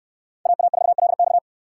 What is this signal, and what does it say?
Morse code, heard as a keyed tone, about 67 wpm, 700 Hz. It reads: AA4LX